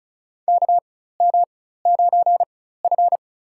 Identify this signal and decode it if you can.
Morse code, heard as a keyed tone, about 36 wpm, spaced as for 16 wpm, 690 Hz. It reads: KM9F
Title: USA Call Signs - 4x Character Spacing - 35wpm